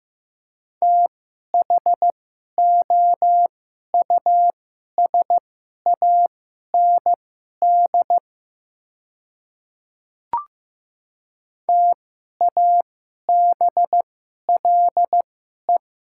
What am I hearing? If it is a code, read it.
Morse code, heard as a keyed tone, about 15 wpm, spaced as for 11 wpm, 700 Hz. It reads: THOUSAND TABLE
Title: 2x Character Spacing - Top 500 Words - 15wpm